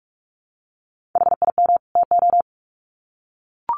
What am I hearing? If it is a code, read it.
Morse code, heard as a keyed tone, about 44 wpm, 700 Hz. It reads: HIM TO
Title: Sets of 2 Words 45wpm